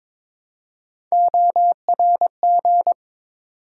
Morse code, heard as a keyed tone, about 22 wpm, 700 Hz. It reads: ORG
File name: ARRL Sections v2 - 22wpm